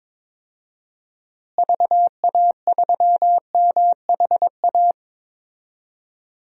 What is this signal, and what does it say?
Morse code, heard as a keyed tone, about 22 wpm, 700 Hz. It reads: VA3MHA
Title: DXCC-100 Call Signs - 6 Characters - Repeated 2x - 22wpm